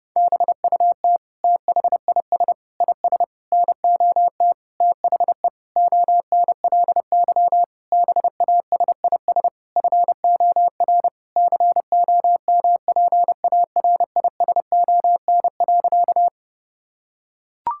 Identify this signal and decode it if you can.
Morse code, heard as a keyed tone, about 30 wpm, 700 Hz. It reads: BUT THIS IS NOT THE ONLY BASIS FOR COMPARISON.
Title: Sentences from Top 1600 Words 30wpm